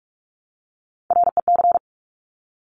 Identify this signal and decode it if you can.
Morse code, heard as a keyed tone, about 46 wpm, 700 Hz. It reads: REC
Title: Top Three Letter Combinations in Words 45wpm